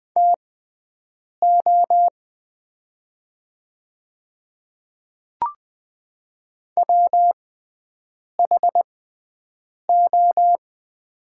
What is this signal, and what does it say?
Morse code, heard as a keyed tone, about 20 wpm, spaced as for 7 wpm, 700 Hz. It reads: TO WHO
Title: 6x Character Spacing - Top 300 Words - 20wpm